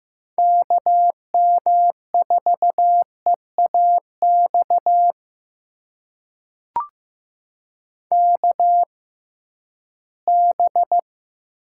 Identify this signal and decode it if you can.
Morse code, heard as a keyed tone, about 15 wpm, spaced as for 11 wpm, 700 Hz. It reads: KM4EAX KB
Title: USA Call Signs - 6 Characters - 6x Character Spacing - 15wpm